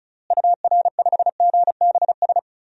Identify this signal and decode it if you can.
Morse code, heard as a keyed tone, about 34 wpm, 700 Hz. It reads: UR5GBS
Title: DXCC-100 Call Signs - 6 Characters - Repeated 2x - 35wpm